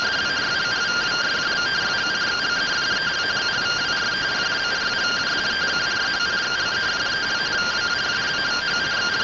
Two-tone teleprinter signal, about 170 Hz shift.